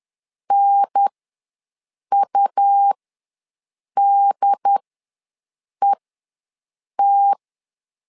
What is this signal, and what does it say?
Morse code, heard as a keyed tone, about 10 wpm, spaced as for 6 wpm, 800 Hz. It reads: NUDET
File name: 2 UHDN ws 10wpm